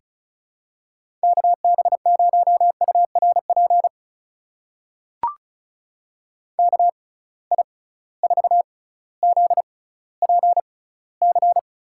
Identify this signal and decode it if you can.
Morse code, heard as a keyed tone, about 36 wpm, spaced as for 18 wpm, 700 Hz. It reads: KB0URP KI4ZPC
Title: USA Call Signs - 6 Characters - 6x Character Spacing - 35wpm